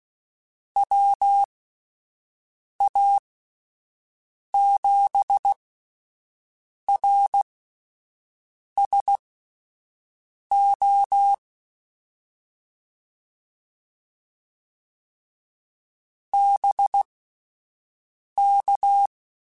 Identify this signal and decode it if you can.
Morse code, heard as a keyed tone, about 16 wpm, spaced as for 5 wpm, 800 Hz. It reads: WA7RSO BK
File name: (spaced) / QSOs Spaced-mp3s / WD0BGP 1 @10 sp